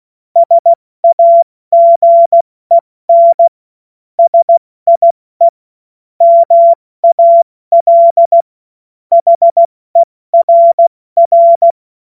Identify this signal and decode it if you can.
Morse code, heard as a keyed tone, about 16 wpm, 680 Hz. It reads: SAGEN SIE MAL HERR
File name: Morsekurs / Witze / Witz 17.Oktober 2023 70 F80